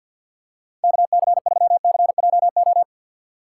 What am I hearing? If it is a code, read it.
Morse code, heard as a keyed tone, about 52 wpm, 700 Hz. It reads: KC3CJO
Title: USA Call Signs - 2x Character Spacing - 50wpm